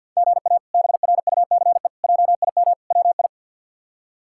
Morse code, heard as a keyed tone, about 51 wpm, 700 Hz. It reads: GA BRUCE JIM WI
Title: K1USN SST QSO Practice - 50wpm